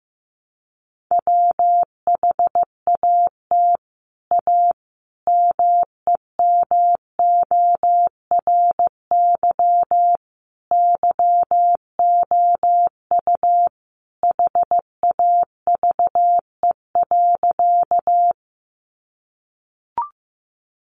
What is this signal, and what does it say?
Morse code, heard as a keyed tone, about 14 wpm, 700 Hz. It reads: WHAT A MEMORY YOU HAVE.
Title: Sentences from Top 1300 Words 15wpm